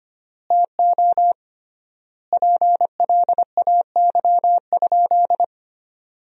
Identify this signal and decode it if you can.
Morse code, heard as a keyed tone, about 25 wpm, 700 Hz. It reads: TO PLAY?
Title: 3x Word-Spacing - Sentences from Top 400 Words - 25wpm